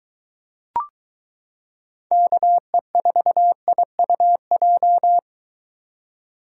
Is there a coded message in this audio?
Yes: Morse code from 2.1 to 5.2 s, heard as a keyed tone, about 23 wpm, 700 Hz. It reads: KE4IUJ